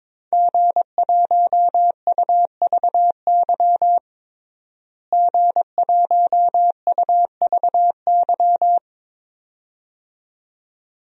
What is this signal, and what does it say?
Morse code, heard as a keyed tone, about 22 wpm, 700 Hz. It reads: G1UVY G1UVY
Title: DXCC-100 Call Signs - 5 Characters - Repeated 2x - 22wpm